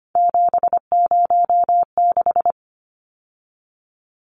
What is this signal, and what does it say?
Morse code, heard as a keyed tone, about 25 wpm, 700 Hz. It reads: 706
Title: Numbers - 3-Digit 25wpm